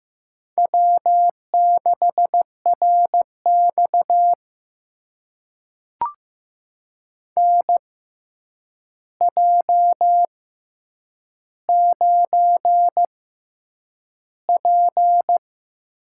Morse code, heard as a keyed tone, about 15 wpm, spaced as for 8 wpm, 700 Hz. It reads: W6RX NJ9P
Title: USA Call Signs - 4 Characters - 6x Character Spacing - 15wpm